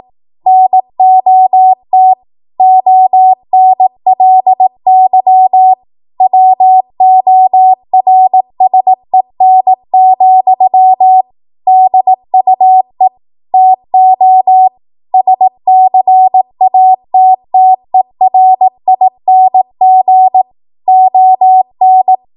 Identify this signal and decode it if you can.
Morse code, heard as a keyed tone, about 18 wpm, 750 Hz. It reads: NOT ONLY WORSEN, DUE TO SCATTERING ON